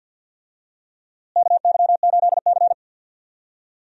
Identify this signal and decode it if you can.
Morse code, heard as a keyed tone, about 51 wpm, 700 Hz. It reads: KY8C
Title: USA Call Signs - 4 Characters - Repeated 3x - 50wpm